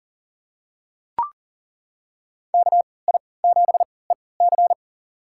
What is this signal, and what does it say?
Morse code, heard as a keyed tone, about 40 wpm, spaced as for 23 wpm, 700 Hz. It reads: KI7EC
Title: USA Call Signs - 3x Character Spacing - 40wpm